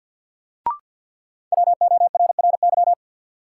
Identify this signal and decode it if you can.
Morse code, heard as a keyed tone, about 48 wpm, 700 Hz. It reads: WORRY
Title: Sets of 5 Words - New Word Drill - 50wpm